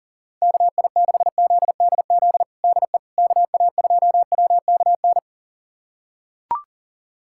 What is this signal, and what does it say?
Morse code, heard as a keyed tone, about 40 wpm, 700 Hz. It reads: KI6ZDZ DE KA2WKN
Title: USA Callsign DE Callsign 40wpm